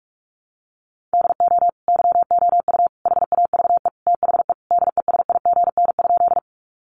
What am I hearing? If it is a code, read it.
Morse code, heard as a keyed tone, about 44 wpm, 700 Hz. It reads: DO YOU HAVE THE DESIGN?
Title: Sentences from Top 800 Words 45wpm